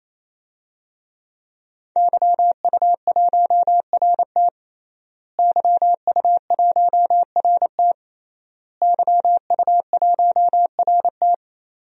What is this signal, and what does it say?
Morse code, heard as a keyed tone, about 28 wpm, 700 Hz. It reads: YU1RT YU1RT YU1RT
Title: DXCC-100 Call Signs - 5 Characters - Repeated 3x - 28wpm